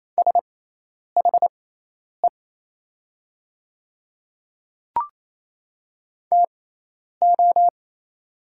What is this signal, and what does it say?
Morse code, heard as a keyed tone, about 28 wpm, spaced as for 10 wpm, 700 Hz. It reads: SHE TO
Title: 6x Character Spacing - Top 100 Words - 28wpm